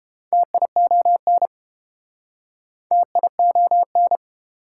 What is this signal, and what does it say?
Morse code, heard as a keyed tone, about 30 wpm, 700 Hz. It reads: TION TION